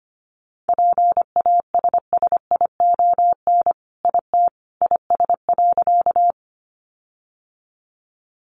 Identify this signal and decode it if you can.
Morse code, heard as a keyed tone, about 25 wpm, 700 Hz. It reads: PASSION IT IS.
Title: Sentences from Top 1300 Words 25wpm